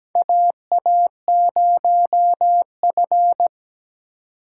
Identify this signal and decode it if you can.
Morse code, heard as a keyed tone, about 17 wpm, 700 Hz. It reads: AA0F
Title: USA Call Signs - 3x Character Spacing - 17wpm